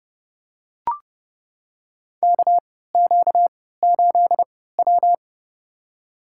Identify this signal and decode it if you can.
Morse code, heard as a keyed tone, about 30 wpm, spaced as for 17 wpm, 700 Hz. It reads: KQ8W
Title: USA Call Signs - 4 Characters - 3x Character Spacing - 30wpm